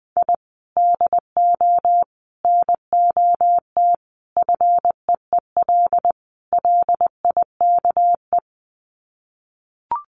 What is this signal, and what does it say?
Morse code, heard as a keyed tone, about 20 wpm, 700 Hz. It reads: I DO NOT FEEL LIKE
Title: Sets of 5 Words Encore 20wpm